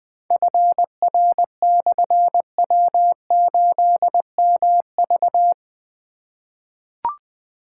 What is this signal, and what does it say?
Morse code, heard as a keyed tone, about 20 wpm, 700 Hz. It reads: FR/W8MV
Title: Difficult Call Signs 20wpm